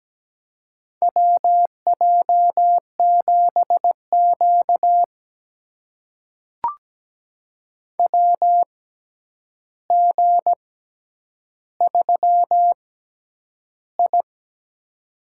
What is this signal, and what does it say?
Morse code, heard as a keyed tone, about 17 wpm, spaced as for 9 wpm, 700 Hz. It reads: WJ7Q WG3I
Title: USA Call Signs - 4 Characters - 6x Character Spacing - 17wpm